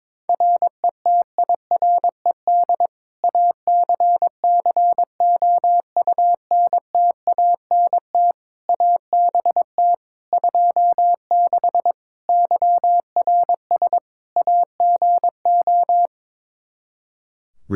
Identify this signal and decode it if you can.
Morse code, heard as a keyed tone, about 22 wpm, 700 Hz. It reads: RETIRED ACCOUNTANT ABT 26 YRS AGO
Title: Practice QSO - Exchange 3 - More Info - 22wpm